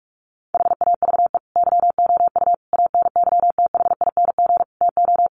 Morse code, heard as a keyed tone, about 44 wpm, 700 Hz. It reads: HAVE YOU ANYTHING TO
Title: Sentences from Top 1100 Words 45wpm